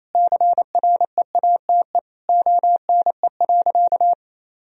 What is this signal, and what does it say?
Morse code, heard as a keyed tone, about 28 wpm, 700 Hz. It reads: CREATE ONE.
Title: Sentences from Top 1000 Words - Review Part 3 - 28wpm